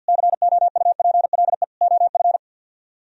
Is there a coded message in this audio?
Yes: Morse code from 0.1 to 2.4 s, heard as a keyed tone, about 64 wpm, 700 Hz. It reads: COUPLE OF